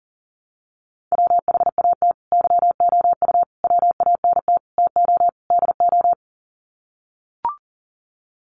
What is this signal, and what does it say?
Morse code, heard as a keyed tone, about 40 wpm, 700 Hz. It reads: WHAT YOU WANT TO DO
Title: Sets of 5 Words Encore 40wpm